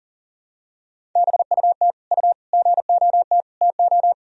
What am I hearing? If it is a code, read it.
Morse code, heard as a keyed tone, about 40 wpm, 700 Hz. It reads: BUT U GOT TO
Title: Sentences from Top 300 Words and Abbreviations 40wpm